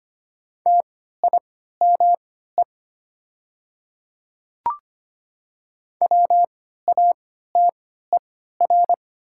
Morse code, heard as a keyed tone, about 25 wpm, spaced as for 14 wpm, 700 Hz. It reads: TIME WATER
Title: 3x Character Spacing - Top 100 Words - 25wpm